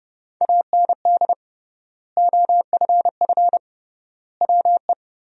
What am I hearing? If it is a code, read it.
Morse code, heard as a keyed tone, about 29 wpm, 700 Hz. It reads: AND OFF WE
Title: 3x Word-Spacing - Sentences from Top 200 Words - 30wpm